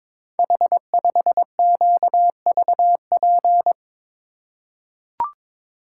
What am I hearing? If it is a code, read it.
Morse code, heard as a keyed tone, about 22 wpm, 700 Hz. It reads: H5QVP
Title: Code Groups - 5 Letter:Numbers - 22wpm